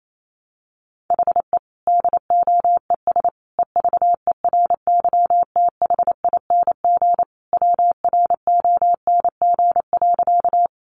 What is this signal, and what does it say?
Morse code, heard as a keyed tone, about 28 wpm, 700 Hz. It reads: HE DOES EVERYTHING WRONG.
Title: Sentences from Top 900 Words 28wpm